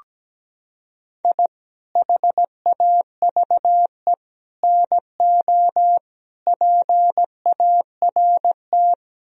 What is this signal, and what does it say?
Morse code, heard as a keyed tone, about 17 wpm, 700 Hz. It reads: I HAVE NO PART
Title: Sentences from Top 1000 Words - Review Part 1 - 17wpm